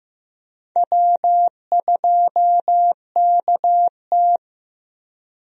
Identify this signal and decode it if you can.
Morse code, heard as a keyed tone, about 15 wpm, 700 Hz. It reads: W2KT
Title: USA Call Signs - 4 Characters - 15wpm